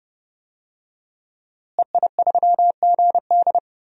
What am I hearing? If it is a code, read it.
Morse code, heard as a keyed tone, about 30 wpm, 700 Hz. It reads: EI3GD
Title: DXCC-100 Call Signs - 5 Characters - Repeated 2x - 30wpm